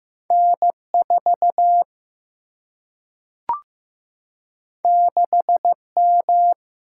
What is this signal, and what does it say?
Morse code, heard as a keyed tone, about 15 wpm, 700 Hz. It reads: N 4 6 M
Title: Single-Letter Single-Number 15wpm